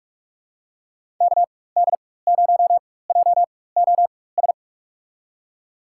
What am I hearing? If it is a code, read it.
Morse code, heard as a keyed tone, about 43 wpm, spaced as for 21 wpm, 700 Hz. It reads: KD0JOS
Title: USA Call Signs - 6 Characters - 4x Character Spacing - 45wpm